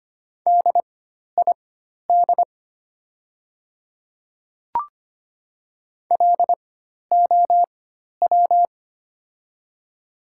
Morse code, heard as a keyed tone, about 25 wpm, spaced as for 12 wpm, 700 Hz. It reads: DID LOW
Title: 4x Character Spacing - Top 200 Words - 25wpm